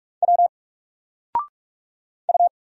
Morse code, heard as a keyed tone, about 46 wpm, 700 Hz. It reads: W U
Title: Single Letter-Number 45wpm